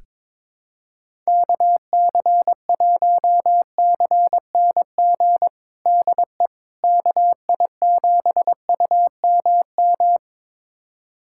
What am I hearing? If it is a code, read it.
Morse code, heard as a keyed tone, about 22 wpm, 700 Hz. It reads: KC1CNG DE KI7UMM
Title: USA Callsign DE Callsign 22wpm